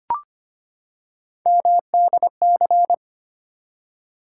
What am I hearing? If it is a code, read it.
Morse code, heard as a keyed tone, about 25 wpm, 700 Hz. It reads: MDC
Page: morse-code-ninja-practice-25wpm